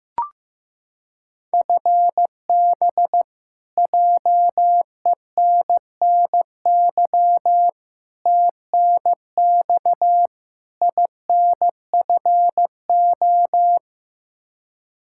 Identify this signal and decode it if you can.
Morse code, heard as a keyed tone, about 15 wpm, 700 Hz. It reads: FB JENNY TNX INFO
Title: Practice QSO - Exchange 3 - More Info - 15wpm